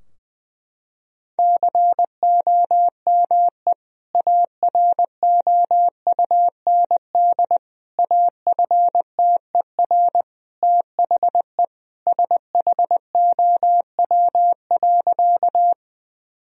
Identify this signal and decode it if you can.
Morse code, heard as a keyed tone, about 21 wpm, 700 Hz. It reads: COME AROUND AFTER THE SHOW.
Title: Sentences from Top 1000 Words - Review Part 3 - 20wpm